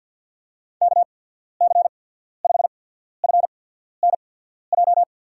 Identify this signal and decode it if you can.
Morse code, heard as a keyed tone, about 53 wpm, spaced as for 14 wpm, 700 Hz. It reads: KC5FNJ
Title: USA Call Signs - 6 Characters - 8x Character Spacing - 50wpm